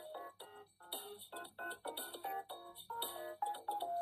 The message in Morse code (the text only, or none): none